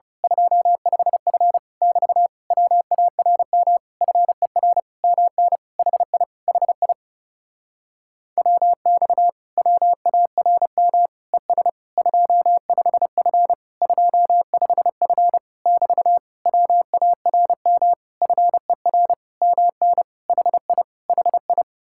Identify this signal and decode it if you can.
Morse code, heard as a keyed tone, about 28 wpm, 700 Hz. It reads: 25F <BT> WARM FER MN HI HI WX WARM ES 25F 25F <BT> WARM FER MN HI HI
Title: Speed-Racing Common QSO Phrases Encore - 30wpm